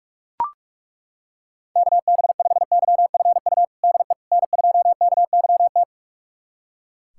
Morse code, heard as a keyed tone, about 42 wpm, 700 Hz. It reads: KB5YFU DE N2KYT
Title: USA Callsign DE Callsign 45wpm